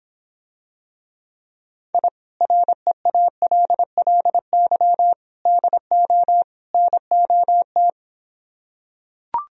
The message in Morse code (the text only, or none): I REALLY DO NOT